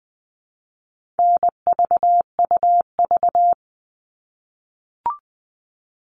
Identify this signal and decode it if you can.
Morse code, heard as a keyed tone, about 20 wpm, 700 Hz. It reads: NVUV
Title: All Cut Numbers - 4-Digit 20wpm